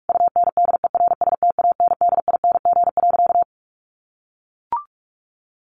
Morse code, heard as a keyed tone, about 44 wpm, 700 Hz. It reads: UNDERSTANDING.
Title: Sentences from Top 1400 Words 45wpm